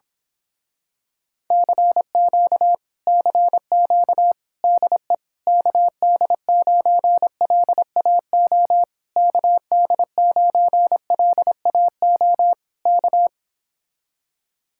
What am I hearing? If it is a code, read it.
Morse code, heard as a keyed tone, about 26 wpm, 700 Hz. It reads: CQ CQ DE KD9LAO KD9LAO K
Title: Speed-Racing Common QSO Phrases - 22wpm